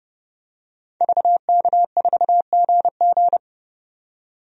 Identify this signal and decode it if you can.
Morse code, heard as a keyed tone, about 30 wpm, 700 Hz. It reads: VK4GG
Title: DXCC-100 Call Signs - 5 Characters - 30wpm